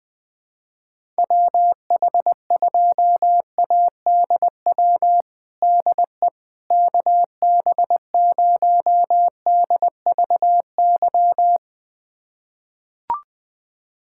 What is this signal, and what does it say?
Morse code, heard as a keyed tone, about 20 wpm, 700 Hz. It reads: WH2ADW DE KB0DVY
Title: USA Callsign DE Callsign - 6 Characters - 20wpm